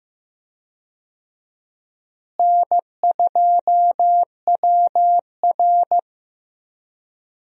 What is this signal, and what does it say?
Morse code, heard as a keyed tone, about 16 wpm, 700 Hz. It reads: N2WR
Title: USA Call Signs - 4 Characters - Repeated 3x - 15wpm